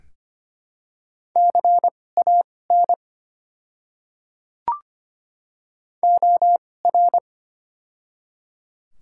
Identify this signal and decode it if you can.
Morse code, heard as a keyed tone, about 25 wpm, spaced as for 18 wpm, 700 Hz. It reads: CAN OR
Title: 2x Character Spacing - Top 100 Words - 25wpm